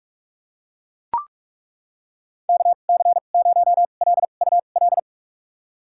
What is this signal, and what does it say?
Morse code, heard as a keyed tone, about 43 wpm, spaced as for 32 wpm, 700 Hz. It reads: KC0LUL
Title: USA Call Signs - 2x Character Spacing - 45wpm